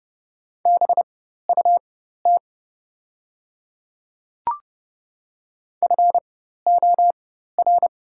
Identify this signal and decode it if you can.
Morse code, heard as a keyed tone, about 30 wpm, spaced as for 14 wpm, 690 Hz. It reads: BUT FOR
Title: 4x Character Spacing - Top 300 Words - 30wpm